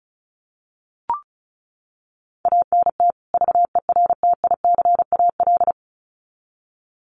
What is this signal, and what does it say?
Morse code, heard as a keyed tone, about 34 wpm, 700 Hz. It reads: ANT VERTICAL
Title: Common QSO Phrases 35wpm